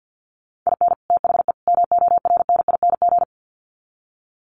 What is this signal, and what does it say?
Morse code, heard as a keyed tone, about 50 wpm, 700 Hz. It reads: IN THE MORNING
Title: Sets of 3 Words 50wpm